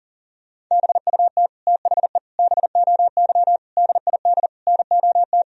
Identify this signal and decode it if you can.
Morse code, heard as a keyed tone, about 40 wpm, 700 Hz. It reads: BUT THE BOY DID NOT
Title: Sentences from Top 1000 Words - Review Part 3 - 40wpm